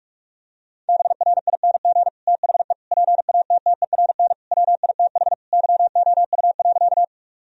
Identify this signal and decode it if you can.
Morse code, heard as a keyed tone, about 46 wpm, 700 Hz. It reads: BRING THE PATTERN WITH YOU.
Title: Sentences from Top 1000 Words - Review Part 2 - 45wpm